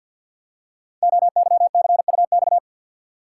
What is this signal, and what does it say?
Morse code, heard as a keyed tone, about 47 wpm, 700 Hz. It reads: OYCUX
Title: Code Groups - 5 Letter:Numbers - 50wpm